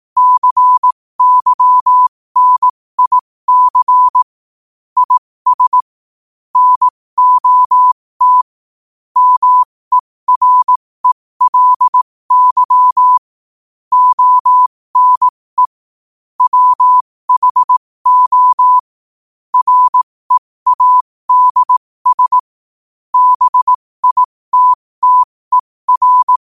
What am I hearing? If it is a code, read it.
Morse code, heard as a keyed tone, about 18 wpm, spaced as for 15 wpm, 1000 Hz. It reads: CYNIC IS NOT MERELY ONE WHO READS BITTER